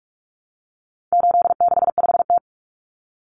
Morse code, heard as a keyed tone, about 44 wpm, 700 Hz. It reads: 865T
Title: Common Cut Numbers - 4-Digit 45wpm